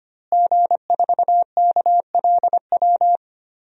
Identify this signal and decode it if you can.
Morse code, heard as a keyed tone, about 25 wpm, 700 Hz. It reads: G4KLW
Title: DXCC-100 Call Signs - 5 Characters - Repeated 3x - 25wpm